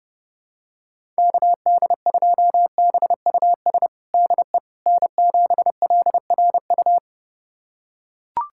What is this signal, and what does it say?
Morse code, heard as a keyed tone, about 30 wpm, 700 Hz. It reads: KD2BUS DE N7LRU
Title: USA Callsign DE Callsign 30wpm